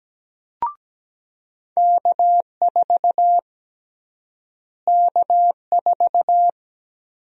Morse code, heard as a keyed tone, about 17 wpm, 700 Hz. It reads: K4 K4
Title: USA Call Sign Prefixes - Repeated 2x - 17wpm